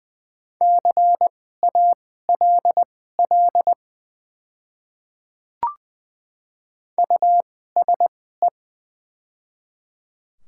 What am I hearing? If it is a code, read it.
Morse code, heard as a keyed tone, about 20 wpm, spaced as for 14 wpm, 690 Hz. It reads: CALL USE
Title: 2x Character Spacing - Top 100 Words - 20wpm